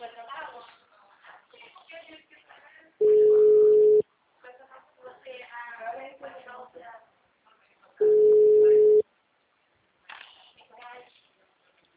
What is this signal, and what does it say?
Ringback tone (Europe, 425 Hz), with speech.